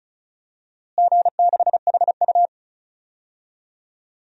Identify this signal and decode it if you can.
Morse code, heard as a keyed tone, about 36 wpm, 700 Hz. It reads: G6HU
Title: DXCC-100 Call Signs - 4 Characters - Repeated 2x - 35wpm